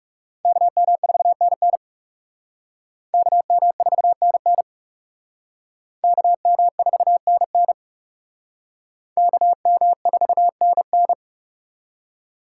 Morse code, heard as a keyed tone, about 36 wpm, 700 Hz. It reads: KM4NN KM4NN KM4NN KM4NN
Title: Speed Racing - USA 2x2 Call Signs 30wpm